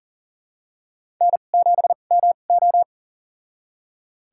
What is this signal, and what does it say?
Morse code, heard as a keyed tone, about 40 wpm, spaced as for 29 wpm, 700 Hz. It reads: N7MO
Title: USA Call Signs - 2x Character Spacing - 40wpm